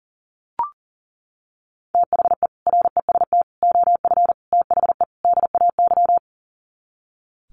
Morse code, heard as a keyed tone, about 40 wpm, 700 Hz. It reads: THE REST OF THE DAY
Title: Sets of 5 Words 40wpm